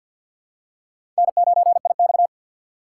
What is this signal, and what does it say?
Morse code, heard as a keyed tone, about 53 wpm, 700 Hz. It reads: N9IX
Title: USA Call Signs - 4x Character Spacing - 50wpm